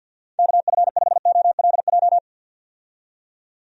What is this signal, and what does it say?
Morse code, heard as a keyed tone, about 49 wpm, 700 Hz. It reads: KF5OLJ
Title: USA Call Signs - 6 Characters - 50wpm